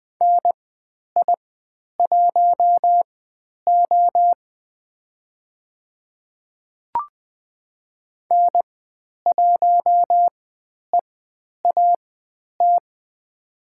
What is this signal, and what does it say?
Morse code, heard as a keyed tone, about 20 wpm, spaced as for 10 wpm, 700 Hz. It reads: NI1O N1EAT
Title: 095 Lesson 16g - Letter c - Call Signs 20wpm